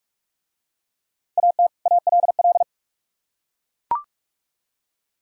Morse code, heard as a keyed tone, about 43 wpm, 700 Hz. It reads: AT ALL